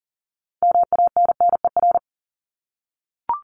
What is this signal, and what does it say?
Morse code, heard as a keyed tone, about 40 wpm, 700 Hz. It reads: MANNER
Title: Top 1501-1750 Words 40wpm